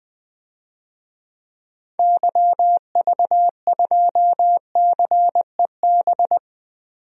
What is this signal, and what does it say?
Morse code, heard as a keyed tone, about 20 wpm, 700 Hz. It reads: YV2CEB
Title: DXCC-100 Call Signs - 6 Characters - 20wpm